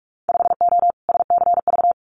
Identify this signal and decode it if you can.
Morse code, heard as a keyed tone, about 46 wpm, 700 Hz. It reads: 5O SCV
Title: Winter Field Day Exchanges Encore 45wpm